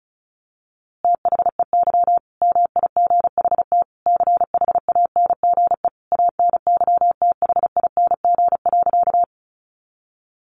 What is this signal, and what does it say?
Morse code, heard as a keyed tone, about 34 wpm, 700 Hz. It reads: THEY MIGHT CHANGE ANYTHING.
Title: A Sound Of Thunder 35wpm